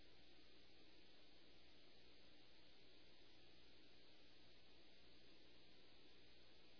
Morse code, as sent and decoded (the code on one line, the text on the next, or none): none